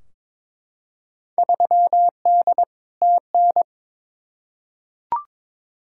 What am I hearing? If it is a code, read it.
Morse code, heard as a keyed tone, about 22 wpm, 700 Hz. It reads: 3D TN
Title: Field Day Exchanges Encore - 22wpm